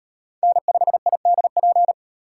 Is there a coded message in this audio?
Yes: Morse code from 0.4 to 1.9 s, heard as a keyed tone, about 37 wpm, 700 Hz. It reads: N5IDP